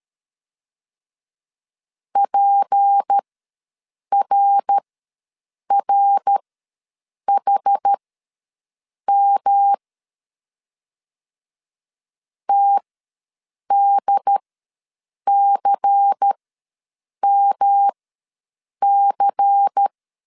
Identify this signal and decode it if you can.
Morse code, heard as a keyed tone, about 13 wpm, spaced as for 7 wpm, 800 Hz. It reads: PRRHM TDCMC
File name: LEZIONI / 6 PKX0 / 6 PKX0 ws 12wpm 3448